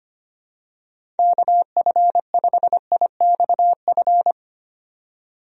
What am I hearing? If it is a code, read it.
Morse code, heard as a keyed tone, about 25 wpm, 700 Hz. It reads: KF5IXF
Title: USA Call Signs - 6 Characters - 25wpm